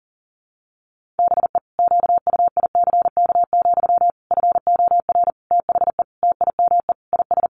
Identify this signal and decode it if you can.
Morse code, heard as a keyed tone, about 40 wpm, 700 Hz. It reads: BE QUICK, FOR THE TIME IS
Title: Sentences from Top 500 Words 40wpm